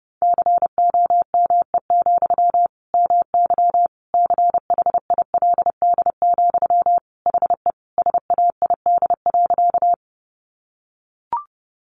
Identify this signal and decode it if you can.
Morse code, heard as a keyed tone, about 30 wpm, 700 Hz. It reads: COME, MY CHILD, HE SAID.
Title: Sentences from Top 800 Words 30wpm